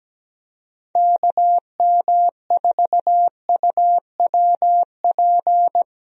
Morse code, heard as a keyed tone, about 17 wpm, 700 Hz. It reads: KM4UWP
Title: USA Call Signs - 6 Characters - Repeated 3x - 17wpm